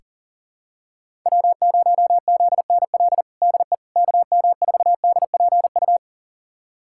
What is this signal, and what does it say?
Morse code, heard as a keyed tone, about 40 wpm, 700 Hz. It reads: W0ZNL DE KM4DPU
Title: USA Callsign DE Callsign 40wpm